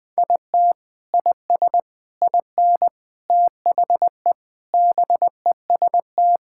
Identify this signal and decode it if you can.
Morse code, heard as a keyed tone, about 21 wpm, 700 Hz. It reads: IT IS IN THE BEST
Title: Sentences with 6-Word Phrases - 20wpm